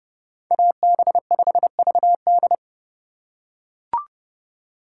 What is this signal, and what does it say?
Morse code, heard as a keyed tone, about 31 wpm, 700 Hz. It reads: AB5VD